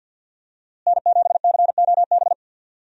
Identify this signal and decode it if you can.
Morse code, heard as a keyed tone, about 51 wpm, 700 Hz. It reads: N7COB
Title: USA Call Signs - 5 Characters - Repeated 3x - 50wpm